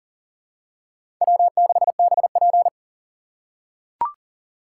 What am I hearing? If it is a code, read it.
Morse code, heard as a keyed tone, about 40 wpm, 700 Hz. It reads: W6BP
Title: USA Call Signs - 4 Characters - 4x Character Spacing - 40wpm